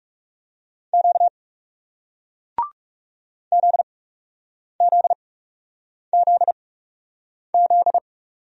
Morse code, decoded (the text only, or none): Q ZZZZ